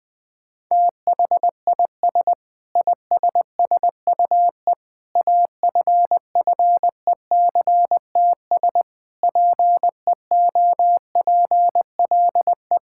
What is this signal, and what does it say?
Morse code, heard as a keyed tone, about 20 wpm, 700 Hz. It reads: THIS ISSUE AFFECTS PEOPLE
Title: Sentences with 8-Word Phrases - 20wpm